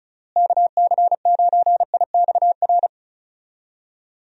Morse code, heard as a keyed tone, about 34 wpm, 700 Hz. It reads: KC9IXR
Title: USA Call Signs - 6 Characters - Repeated 3x - 35wpm